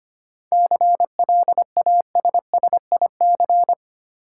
Morse code, heard as a keyed tone, about 25 wpm, 700 Hz. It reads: CLASSIC
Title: Sentences from Top 1000 Words - New Word Drill - 25wpm